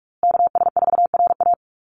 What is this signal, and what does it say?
Morse code, heard as a keyed tone, about 44 wpm, 700 Hz. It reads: KS4RA
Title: USA Call Signs Repeated 2x 45wpm